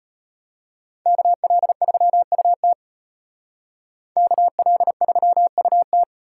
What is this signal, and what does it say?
Morse code, heard as a keyed tone, about 36 wpm, 700 Hz. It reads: KL3UT KL3UT